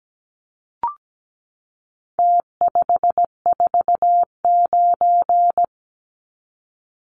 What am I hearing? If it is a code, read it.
Morse code, heard as a keyed tone, about 17 wpm, 700 Hz. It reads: T549
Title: Numbers - Contest Style - 4-Digit 17wpm